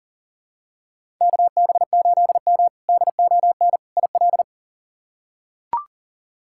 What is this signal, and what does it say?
Morse code, heard as a keyed tone, about 40 wpm, 700 Hz. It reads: KB8M DON IL
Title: K1USN SST QSO Practice - 40wpm